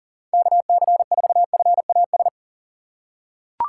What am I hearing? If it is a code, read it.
Morse code, heard as a keyed tone, about 37 wpm, 700 Hz. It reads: KC4FAS
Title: USA Call Signs - 6 Characters - 6x Character Spacing - 40wpm